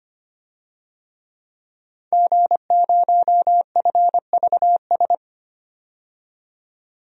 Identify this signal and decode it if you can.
Morse code, heard as a keyed tone, about 25 wpm, 700 Hz. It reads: G0FVS
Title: DXCC-100 Call Signs - 5 Characters - 25wpm